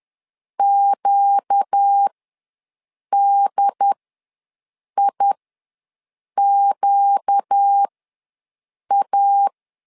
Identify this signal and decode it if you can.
Morse code, heard as a keyed tone, about 10 wpm, spaced as for 6 wpm, 800 Hz. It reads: QDIQA
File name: LEZIONI / 4 Q ws 10wpm 2306